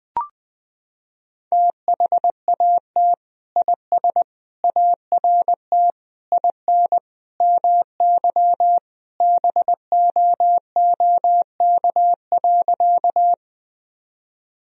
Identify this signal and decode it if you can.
Morse code, heard as a keyed tone, about 20 wpm, 700 Hz. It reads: THAT IS ART IN MY BOOK.
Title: Sentences from Top 1000 Words - Review Part 3 - 20wpm